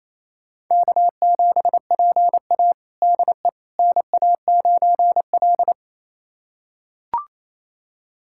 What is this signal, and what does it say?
Morse code, heard as a keyed tone, about 28 wpm, 700 Hz. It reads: K7PA DE NA9L
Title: USA Callsign DE Callsign - 4 Characters - 28wpm